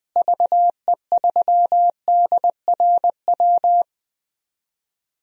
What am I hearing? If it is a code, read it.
Morse code, heard as a keyed tone, about 20 wpm, 700 Hz. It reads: VE3DRW
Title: DXCC-100 Call Signs - 6 Characters - Repeated 3x - 20wpm